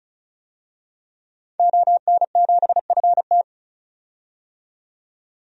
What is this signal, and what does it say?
Morse code, heard as a keyed tone, about 36 wpm, 700 Hz. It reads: ON7FT
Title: DXCC-100 Call Signs - 5 Characters - 35wpm